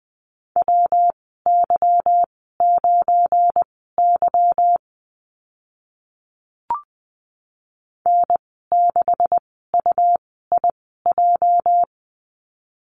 Morse code, heard as a keyed tone, about 20 wpm, spaced as for 14 wpm, 700 Hz. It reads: WY9Y N6UIJ
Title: 231 Lesson 37g - Letter x - Call Signs 20wpm